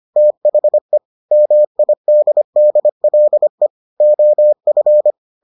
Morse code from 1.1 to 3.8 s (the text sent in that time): MIDDLE